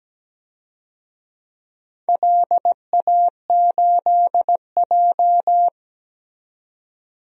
Morse code, heard as a keyed tone, about 17 wpm, 700 Hz. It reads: LA8J
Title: DXCC-100 Call Signs - 4 Characters - 17wpm